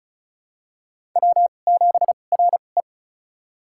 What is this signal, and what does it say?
Morse code, heard as a keyed tone, about 36 wpm, spaced as for 25 wpm, 700 Hz. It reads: W7RE
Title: USA Call Signs - 4 Characters - 2x Character Spacing - 35wpm